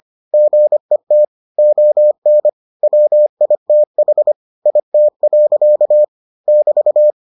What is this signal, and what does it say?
Morse code, heard as a keyed tone, about 25 wpm, 600 Hz. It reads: GET ON WITH IT. <BT>